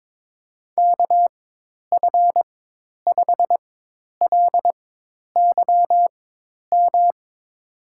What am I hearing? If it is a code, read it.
Morse code, heard as a keyed tone, about 22 wpm, spaced as for 10 wpm, 700 Hz. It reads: KF5LYM
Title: USA Call Signs - 6 Characters - 4x Character Spacing - 22wpm